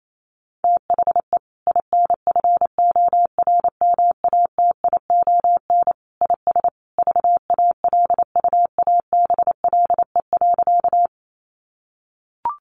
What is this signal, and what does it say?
Morse code, heard as a keyed tone, about 28 wpm, 700 Hz. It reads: THE INFORMATION IS VALUABLE.
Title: Sentences from Top 1100 Words 28wpm